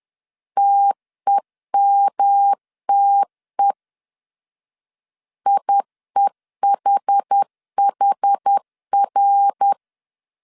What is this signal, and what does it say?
Morse code, heard as a keyed tone, about 10 wpm, 800 Hz. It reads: TEMTE IEHHR
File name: CW / DET / 1 3 1500C S B 10wpm 3335